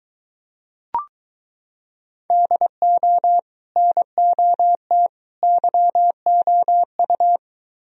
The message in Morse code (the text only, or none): DO NOT YOU